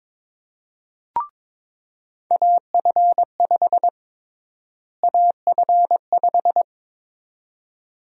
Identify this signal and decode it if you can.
Morse code, heard as a keyed tone, about 22 wpm, 700 Hz. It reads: AF5 AF5
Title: USA Call Sign Prefixes - Repeated 2x - 22wpm